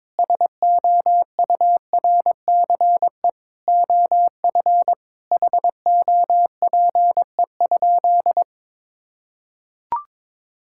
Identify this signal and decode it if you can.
Morse code, heard as a keyed tone, about 22 wpm, 700 Hz. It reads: SOURCE OF HOPE?
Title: Sentences from Top 1000 Words - Review Part 5 - 22wpm